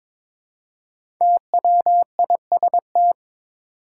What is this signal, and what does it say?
Morse code, heard as a keyed tone, about 22 wpm, 700 Hz. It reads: TWIST
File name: 5 Letter Words - First 3 Letters Given - 22wpm